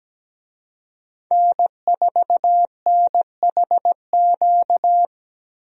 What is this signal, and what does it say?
Morse code, heard as a keyed tone, about 17 wpm, 700 Hz. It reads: N4NHQ
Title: USA Call Signs - 5 Characters - 17wpm